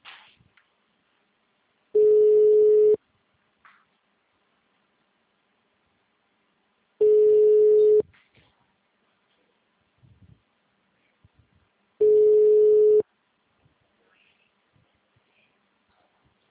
Ringback tone (Europe, 425 Hz).
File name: out- -2011- - - .78722